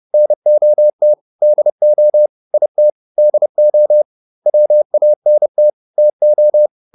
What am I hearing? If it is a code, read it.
Morse code, heard as a keyed tone, about 30 wpm, 600 Hz. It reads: NOT DO IT DO WANT TO